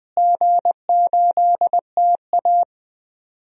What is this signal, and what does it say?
Morse code, heard as a keyed tone, about 20 wpm, 700 Hz. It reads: G8TA
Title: DXCC-100 Call Signs - 4 Characters - Repeated 3x - 20wpm